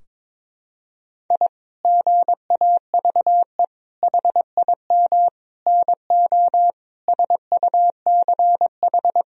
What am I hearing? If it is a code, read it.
Morse code, heard as a keyed tone, about 22 wpm, 700 Hz. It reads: I GAVE HIM NO SUCH
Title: Sentences from Top 1000 Words - Review Part 2 - 22wpm